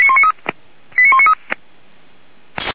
Five-tone selective call (ZVEI), heard as quick runs of five tones, tones of about 70 ms. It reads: 89193